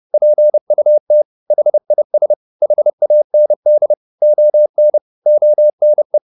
Morse code, heard as a keyed tone, about 30 wpm, 600 Hz. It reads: PUT HIS HAND ON ONE
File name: the old man and the sea / the old man and the sea 1 30wpm